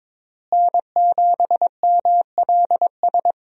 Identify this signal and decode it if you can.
Morse code, heard as a keyed tone, about 22 wpm, 700 Hz. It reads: N7MLS
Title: USA Call Signs - 5 Characters - Repeated 2x - 22wpm